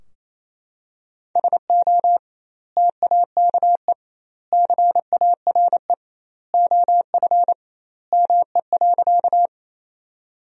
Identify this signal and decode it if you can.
Morse code, heard as a keyed tone, about 27 wpm, 700 Hz. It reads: SO TAKE CARE OF ME.
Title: 2x Word-Spacing - Sentences from Top 300 Words - 28wpm